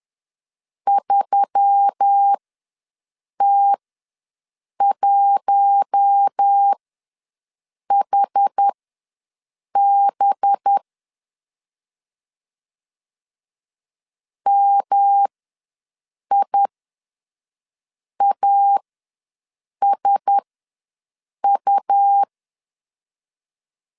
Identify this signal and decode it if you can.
Morse code, heard as a keyed tone, about 10 wpm, spaced as for 6 wpm, 800 Hz. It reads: 3T1HB MIASU
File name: LEZIONI / 5 LBVC / 5 LBV ws 10wpm 3645